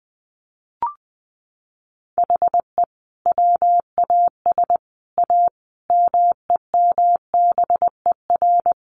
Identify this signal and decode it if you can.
Morse code, heard as a keyed tone, about 20 wpm, 700 Hz. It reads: HE WAS A MEMBER
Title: Sets of 5 Words Encore2 20wpm